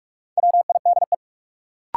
Morse code, heard as a keyed tone, about 42 wpm, 700 Hz. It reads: WIDE
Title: Sentences from Top 900 Words - New Word Drill - 45wpm